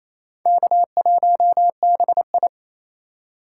Morse code, heard as a keyed tone, about 36 wpm, 700 Hz. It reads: K1BI